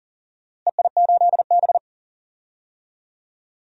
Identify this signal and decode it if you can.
Morse code, heard as a keyed tone, about 40 wpm, 700 Hz. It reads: EI8B
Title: DXCC-100 Call Signs - 4 Characters - Repeated 3x - 40wpm